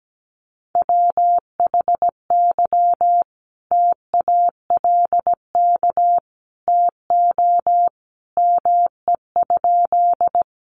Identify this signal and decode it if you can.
Morse code, heard as a keyed tone, about 17 wpm, 700 Hz. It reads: WHY TALK TO ME?
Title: Sentences from Top 400 Words 17wpm